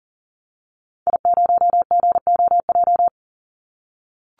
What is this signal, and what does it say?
Morse code, heard as a keyed tone, about 40 wpm, 700 Hz. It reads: I0GOJ
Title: Call Signs - 40wpm